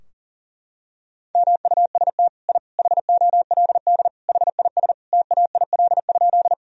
Morse code, heard as a keyed tone, about 40 wpm, 700 Hz. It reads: MUST I HOLD HIS TAIL?
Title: Sentences from Top 1000 Words - Review Part 3 - 40wpm